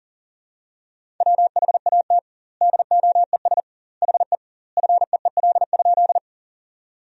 Morse code, heard as a keyed tone, about 37 wpm, 700 Hz. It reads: WHAT DOES HE FEEL?
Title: 2x Word-Spacing - Sentences from Top 400 Words - 40wpm